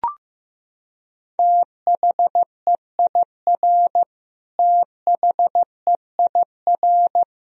Morse code, heard as a keyed tone, about 15 wpm, 700 Hz. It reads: THEIR THEIR
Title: Top 100 Words Repeated 15wpm